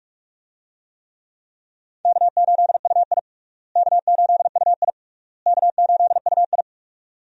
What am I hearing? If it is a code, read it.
Morse code, heard as a keyed tone, about 46 wpm, 700 Hz. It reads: K8UI K8UI K8UI
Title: USA Call Signs - 4 Characters - Repeated 3x - 45wpm